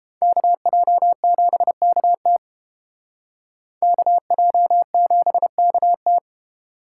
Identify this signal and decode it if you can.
Morse code, heard as a keyed tone, about 31 wpm, 700 Hz. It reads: KJ7KT KJ7KT